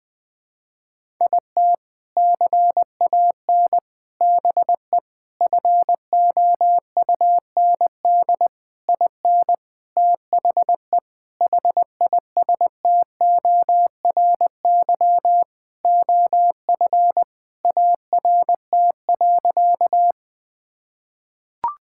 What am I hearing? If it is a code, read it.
Morse code, heard as a keyed tone, about 21 wpm, 700 Hz. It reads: IT CAN BE FOUND IN THE HISTORY OF ART.
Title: Sentences with 4-Word Phrases - Spoken Beginning - 20wpm